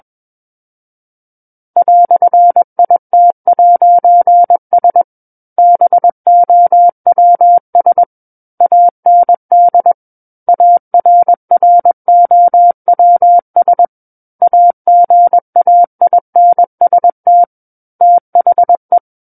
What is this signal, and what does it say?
Morse code, heard as a keyed tone, about 21 wpm, 700 Hz. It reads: "IT'S BOWS AND ARROWS AGAINST THE
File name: Book One / 12-Chapter 12 - 21 WPM